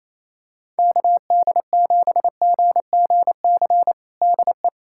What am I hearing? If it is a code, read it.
Morse code, heard as a keyed tone, about 27 wpm, 700 Hz. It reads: KD7GGC DE
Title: USA Callsign DE Callsign - 6 Characters - 28wpm